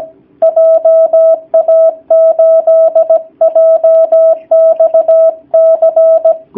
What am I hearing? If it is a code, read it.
Morse code, heard as a keyed tone, about 17 wpm, 650 Hz. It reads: JA8JXC
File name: jh8cbh / 2019cw / sp16- w